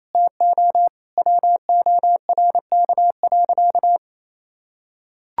Morse code, heard as a keyed tone, about 28 wpm, 700 Hz. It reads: TO WORK.
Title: Sentences from Top 1000 Words - Review Part 5 - 28wpm